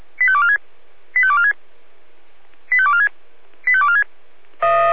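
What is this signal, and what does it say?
Five-tone selective call (ZVEI), heard as quick runs of five tones, tones of about 70 ms. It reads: 85246 85246 85246 85246
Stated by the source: ZVEI five-tone call received by SDR.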